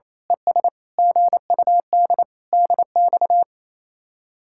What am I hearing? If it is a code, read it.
Morse code, heard as a keyed tone, about 28 wpm, 700 Hz. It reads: ES GUD DX
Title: Speed-Racing Common QSO Phrases - 28wpm